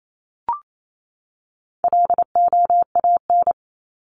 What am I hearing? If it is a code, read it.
Morse code, heard as a keyed tone, about 28 wpm, 700 Hz. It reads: LOAN
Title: Top 1501-1750 Words 28wpm